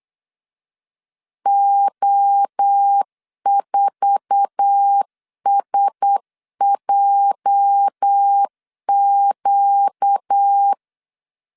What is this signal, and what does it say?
Morse code, heard as a keyed tone, about 8 wpm, 800 Hz. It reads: O4SJQ